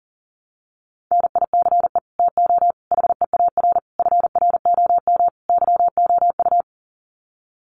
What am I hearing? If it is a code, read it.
Morse code, heard as a keyed tone, about 40 wpm, 700 Hz. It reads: NICE TO HEAR FROM YOU
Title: Common QSO Phrases 40wpm